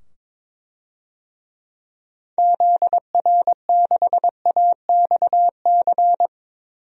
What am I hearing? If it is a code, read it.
Morse code, heard as a keyed tone, about 22 wpm, 700 Hz. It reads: ZR6AXC
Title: DXCC-100 Call Signs - 6 Characters - 22wpm